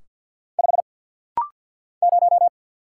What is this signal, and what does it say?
Morse code, heard as a keyed tone, about 50 wpm, 700 Hz. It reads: 5 0
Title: Single Letter-Number - Rapid-Fire 50wpm